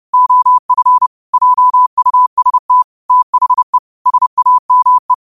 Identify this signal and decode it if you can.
Morse code, heard as a keyed tone, about 30 wpm, 1000 Hz. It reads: OF JUST THE SAME
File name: qotdfiles / 30wpm